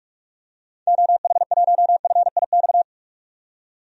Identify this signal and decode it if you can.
Morse code, heard as a keyed tone, about 46 wpm, 700 Hz. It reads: OH1FIX
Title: DXCC-100 Call Signs - 6 Characters - 45wpm